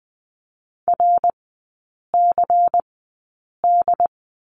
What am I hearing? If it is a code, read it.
Morse code, heard as a keyed tone, about 20 wpm, spaced as for 8 wpm, 700 Hz. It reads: RCD
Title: 159 Lesson 26e - Letter g - Three Characters 20wpm